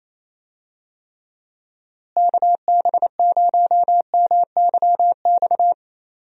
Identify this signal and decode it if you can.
Morse code, heard as a keyed tone, about 28 wpm, 700 Hz. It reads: KB0MYX
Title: USA Call Signs - 6 Characters - Repeated 3x - 28wpm